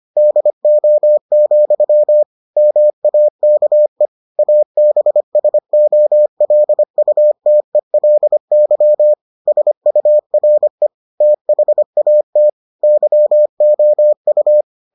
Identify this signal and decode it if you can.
Morse code, heard as a keyed tone, about 25 wpm, 600 Hz. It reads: DO, MAKE ABSOLUTELY SURE THAT YOU